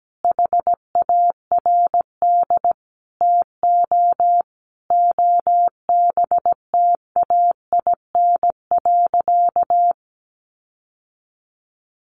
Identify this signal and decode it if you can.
Morse code, heard as a keyed tone, about 17 wpm, 700 Hz. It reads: HARD TO OBTAIN.
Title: Sentences from Top 1300 Words 17wpm